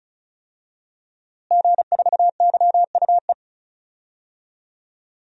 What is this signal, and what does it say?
Morse code, heard as a keyed tone, about 34 wpm, 700 Hz. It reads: G4YUE
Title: DXCC-100 Call Signs - 5 Characters - Repeated 2x - 35wpm